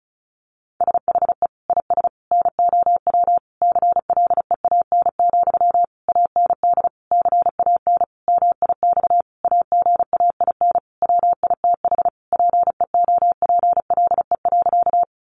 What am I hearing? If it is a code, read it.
Morse code, heard as a keyed tone, about 34 wpm, 700 Hz. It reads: SHE IS NOW CLEAN, AND CAN MIX AGAIN WITH PEOPLE.
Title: Sentences from Top 1100 Words 35wpm